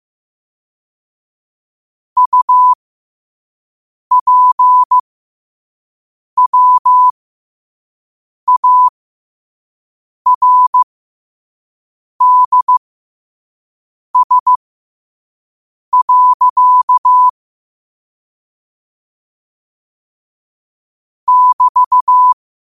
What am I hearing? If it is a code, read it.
Morse code, heard as a keyed tone, about 15 wpm, spaced as for 5 wpm, 1000 Hz. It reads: UPWARDS. <BT>
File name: qotdfiles / 5wpm